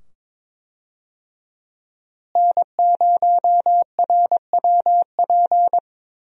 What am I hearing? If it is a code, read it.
Morse code, heard as a keyed tone, about 22 wpm, 700 Hz. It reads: N0RWP
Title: USA Call Signs - 5 Characters - Repeated 2x - 22wpm